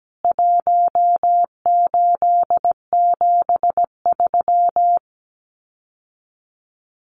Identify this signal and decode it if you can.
Morse code, heard as a keyed tone, about 17 wpm, 700 Hz. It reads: 1873
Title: CWT Exchange 17wpm